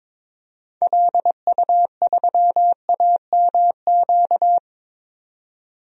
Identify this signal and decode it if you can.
Morse code, heard as a keyed tone, about 22 wpm, 700 Hz. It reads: LU3AMQ
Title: DXCC-100 Call Signs - 6 Characters - 22wpm